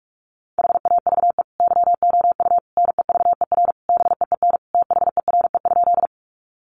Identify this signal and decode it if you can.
Morse code, heard as a keyed tone, about 44 wpm, 700 Hz. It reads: HAVE YOU NEVER BEEN THERE?
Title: Sentences from Top 300 Words 45wpm